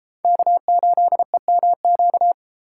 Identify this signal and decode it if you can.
Morse code, heard as a keyed tone, about 32 wpm, 700 Hz. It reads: K8EMQ